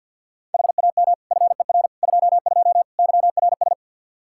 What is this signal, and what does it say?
Morse code, heard as a keyed tone, about 52 wpm, 700 Hz. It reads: HAM FER 22 YRS
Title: Practice QSO - Exchange 3 - More Info - 50wpm